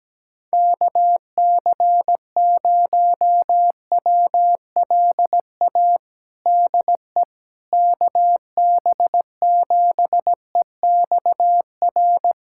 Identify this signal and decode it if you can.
Morse code, heard as a keyed tone, about 17 wpm, 700 Hz. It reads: KC0WLA DE KB7EXR
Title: USA Callsign DE Callsign 17wpm